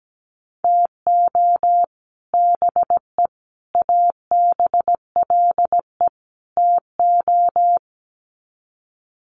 Morse code, heard as a keyed tone, about 17 wpm, 700 Hz. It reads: TO BE ABLE TO
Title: Sets of 4 Words 17wpm